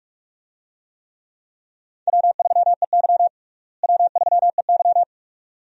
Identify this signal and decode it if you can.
Morse code, heard as a keyed tone, about 46 wpm, 700 Hz. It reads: W3EY W3EY
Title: USA Call Signs - 4 Characters - Repeated 2x - 45wpm